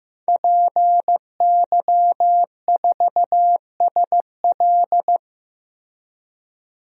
Morse code, heard as a keyed tone, about 15 wpm, 700 Hz. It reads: PY4SL
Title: DXCC-100 Call Signs - 5 Characters - Repeated 3x - 15wpm